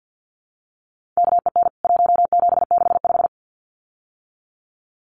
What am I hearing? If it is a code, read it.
Morse code, heard as a keyed tone, about 50 wpm, 700 Hz. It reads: KEN 1765
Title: CWT Exchange 50wpm